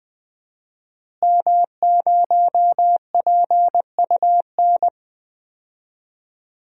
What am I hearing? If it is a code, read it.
Morse code, heard as a keyed tone, about 20 wpm, 700 Hz. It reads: M0PUN
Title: DXCC-100 Call Signs - 5 Characters - Repeated 2x - 20wpm